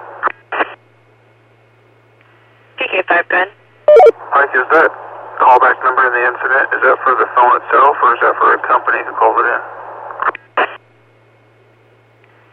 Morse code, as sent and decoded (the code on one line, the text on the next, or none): none